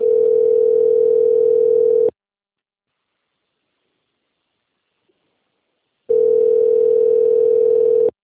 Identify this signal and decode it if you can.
Ringback tone (North America), 440 + 480 Hz.